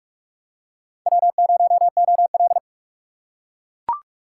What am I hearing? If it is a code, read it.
Morse code, heard as a keyed tone, about 40 wpm, 700 Hz. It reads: W0OL